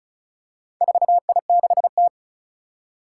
Morse code, heard as a keyed tone, about 36 wpm, 700 Hz. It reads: 4I6T
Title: Four Challenging Letter-Number Combinations 35wpm